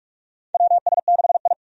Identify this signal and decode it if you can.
Morse code, heard as a keyed tone, about 43 wpm, 700 Hz. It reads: WS6I
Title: USA Call Signs - 4 Characters - 45wpm